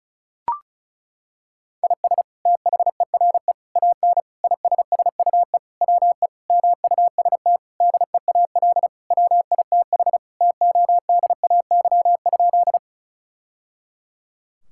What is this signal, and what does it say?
Morse code, heard as a keyed tone, about 34 wpm, 700 Hz. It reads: IS THERE AN ISSUE WE MUST DEAL WITH TODAY?
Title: Sentences from Top 1000 Words - Review Part 6 - 35wpm